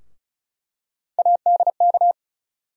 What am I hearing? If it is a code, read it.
Morse code, heard as a keyed tone, about 36 wpm, 700 Hz. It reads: ADK
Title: USA Call Sign Postfixes - 3 Letters 35wpm